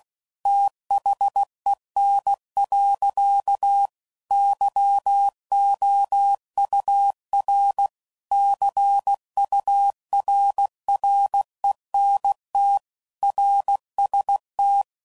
Morse code, heard as a keyed tone, about 16 wpm, 800 Hz. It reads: THEN. YOUR CURRENT RST
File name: CW Code Groups @15WPM / QSOs / WA7QOX